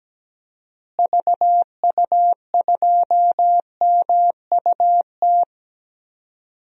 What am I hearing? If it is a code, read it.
Morse code, heard as a keyed tone, about 17 wpm, 700 Hz. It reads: VU2MUT
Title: DXCC-100 Call Signs - 6 Characters - Repeated 2x - 17wpm